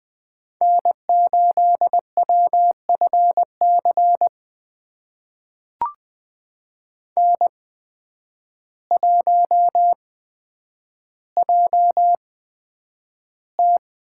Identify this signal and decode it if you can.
Morse code, heard as a keyed tone, about 20 wpm, spaced as for 9 wpm, 700 Hz. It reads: N8WFC N1JT
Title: USA Call Signs - 5 Characters - 8x Character Spacing - 20wpm